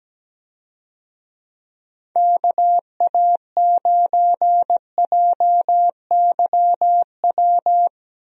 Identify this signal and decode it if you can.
Morse code, heard as a keyed tone, about 17 wpm, 700 Hz. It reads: KA9JYW
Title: USA Call Signs - 6 Characters - 17wpm